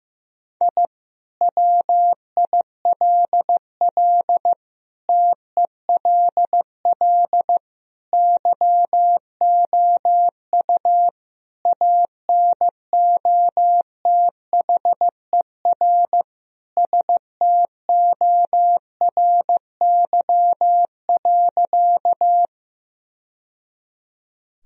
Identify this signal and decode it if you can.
Morse code, heard as a keyed tone, about 15 wpm, 700 Hz. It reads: I WILL TELL YOU ANOTHER STORY.
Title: Sentences from Top 1000 Words - Review Part 1 - 15wpm